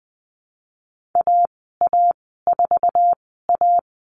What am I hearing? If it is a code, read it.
Morse code, heard as a keyed tone, about 20 wpm, spaced as for 14 wpm, 700 Hz. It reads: AA4A
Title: 055 Lesson 10g - Letter r - Call Signs 20wpm